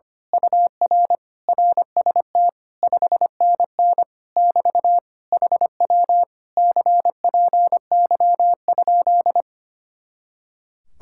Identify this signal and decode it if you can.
Morse code, heard as a keyed tone, about 24 wpm, 700 Hz. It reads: UR RST 5NN <BT> HW CPY?
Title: Speed-Racing Common QSO Phrases - 25wpm